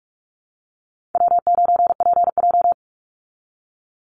Morse code, heard as a keyed tone, about 44 wpm, 700 Hz. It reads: W9PJ
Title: Easy Call Signs Repeated 3x - 45wpm